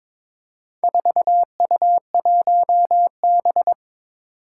Morse code, heard as a keyed tone, about 22 wpm, 700 Hz. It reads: 4U1B
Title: Four Challenging Letter-Number Combinations 22wpm